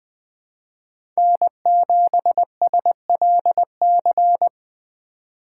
Morse code, heard as a keyed tone, about 21 wpm, 700 Hz. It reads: N7SLC
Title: USA Call Signs - 5 Characters - 2x Character Spacing - 20wpm